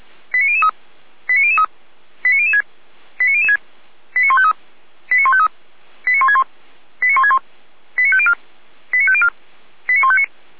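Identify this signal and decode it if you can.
Five-tone selective call (ZVEI), heard as quick runs of five tones, tones of about 80 ms. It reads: 89003 89003 89006 89006 89153 89153 89161 89161 89504 89504 89159
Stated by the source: ZVEI five-tone call 89099 received by SDR, from the Feuerwehr Orlamünde fire brigade audio archive.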